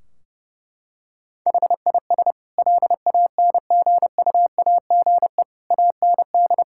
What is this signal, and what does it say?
Morse code, heard as a keyed tone, about 30 wpm, 700 Hz. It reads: HIS LANGUAGE AND
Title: Sentences from Top 1600 Words 30wpm